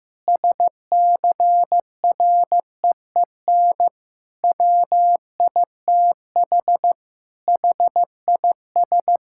Morse code, heard as a keyed tone, about 16 wpm, 700 Hz. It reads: SCREEN WITH HIS
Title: Sentences from Top 1000 Words - Review Part 6 - 15wpm